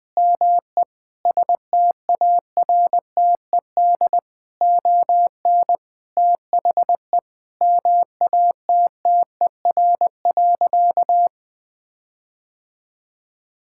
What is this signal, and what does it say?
Morse code, heard as a keyed tone, about 21 wpm, 700 Hz. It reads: ME STARTED ON THE MATTER.
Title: Sentences with 7-Word Phrases - 20wpm